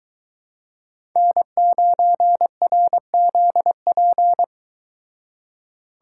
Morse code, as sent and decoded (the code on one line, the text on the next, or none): -. ----. .-. --.. .--.
N9RZP